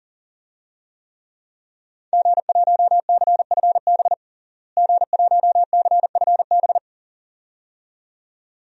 Morse code, heard as a keyed tone, about 40 wpm, 700 Hz. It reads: G1CFB G1CFB
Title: DXCC-100 Call Signs - 5 Characters - Repeated 2x - 40wpm